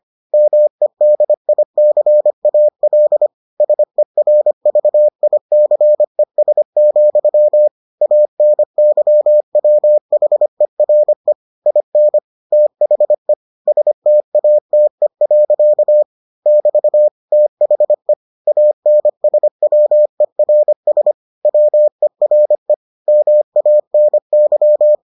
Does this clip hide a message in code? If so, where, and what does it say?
Yes: Morse code from 0.3 to 24.9 s, heard as a keyed tone, about 25 wpm, 600 Hz. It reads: MEDICAL SERVICES, ANYWHERE IN THE STATE. <BT> THE ANSWERS WERE MANY